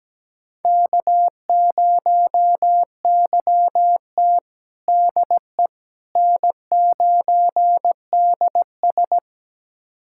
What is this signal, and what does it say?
Morse code, heard as a keyed tone, about 17 wpm, 700 Hz. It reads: K0YT DE N9DS
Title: USA Callsign DE Callsign - 4 Characters - 17wpm